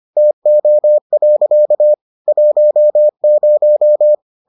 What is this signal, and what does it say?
Morse code, heard as a keyed tone, about 25 wpm, 600 Hz. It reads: TO. 10